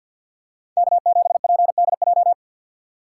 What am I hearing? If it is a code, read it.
Morse code, heard as a keyed tone, about 50 wpm, 700 Hz. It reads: K7PDJ
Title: USA Call Signs - 5 Characters - 2x Character Spacing - 50wpm